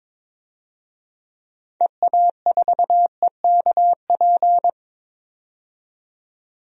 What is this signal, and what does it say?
Morse code, heard as a keyed tone, about 22 wpm, 700 Hz. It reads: EA4EKP
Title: DXCC-100 Call Signs - 6 Characters - 22wpm